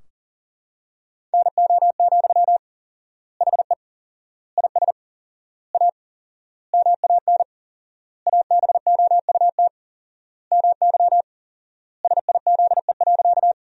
Morse code, heard as a keyed tone, about 40 wpm, 700 Hz. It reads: NO, HE IS A MAN ABOUT MY SIZE.
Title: 4x Word-Spacing - Sentences from Top 400 Words - 40wpm